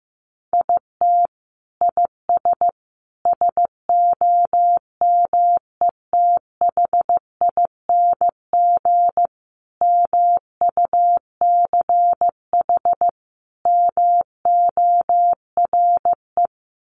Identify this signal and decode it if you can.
Morse code, heard as a keyed tone, about 15 wpm, 700 Hz. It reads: IT IS SOMETHING MUCH MORE
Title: Sentences from Top 1100 Words 15wpm